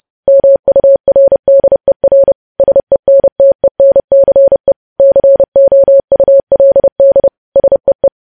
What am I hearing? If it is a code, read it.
Morse code, heard as a keyed tone, about 30 wpm, 550 Hz. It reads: MURDER SENTENCE COULD SEE
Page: news-headlines-in-morse-code-at-30-wpm